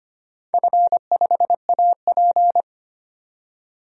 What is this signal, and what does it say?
Morse code, heard as a keyed tone, about 25 wpm, 700 Hz. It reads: F5AP
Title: DXCC-100 Call Signs - 4 Characters - 25wpm